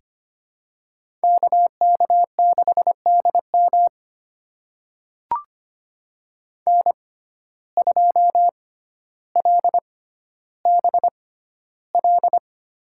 Morse code, heard as a keyed tone, about 25 wpm, spaced as for 13 wpm, 700 Hz. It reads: KK6DM N2LBL
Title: USA Call Signs - 5 Characters - 6x Character Spacing - 25wpm